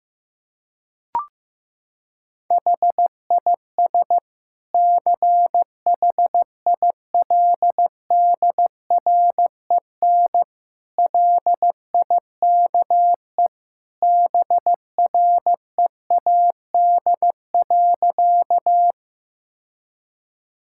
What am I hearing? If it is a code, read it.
Morse code, heard as a keyed tone, about 15 wpm, 700 Hz. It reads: HIS CHILDREN LIKE BREAD.
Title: Sentences from Top 1600 Words 15wpm